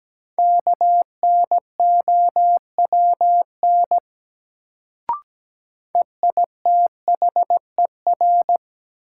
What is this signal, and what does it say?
Morse code, heard as a keyed tone, about 17 wpm, 700 Hz. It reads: KNOWN EITHER
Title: Sentences from Top 600 Words - New Word Drill - 17wpm